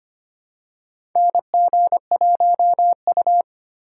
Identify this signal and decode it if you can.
Morse code, heard as a keyed tone, about 25 wpm, 700 Hz. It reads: NG1U
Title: USA Call Signs - 4 Characters - 4x Character Spacing - 25wpm